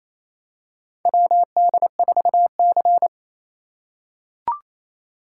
Morse code, heard as a keyed tone, about 28 wpm, 700 Hz. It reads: WD4C
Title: USA Call Signs - 4 Characters - 4x Character Spacing - 28wpm